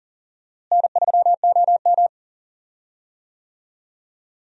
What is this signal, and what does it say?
Morse code, heard as a keyed tone, about 40 wpm, 700 Hz. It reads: N3OM
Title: USA Call Signs - 4 Characters - Repeated 3x - 40wpm